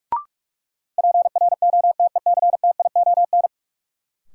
Morse code, heard as a keyed tone, about 42 wpm, 700 Hz. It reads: PROTECTION
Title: Sentences from Top 1600 Words - New Word Drill - 45wpm